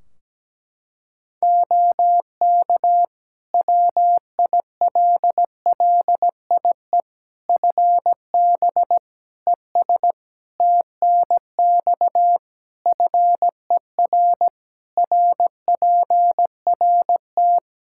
Morse code, heard as a keyed tone, about 17 wpm, 700 Hz. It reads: OK WILLIE FB ES TNX FER RPRT
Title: Practice QSO - Exchange 2 - Info - 17wpm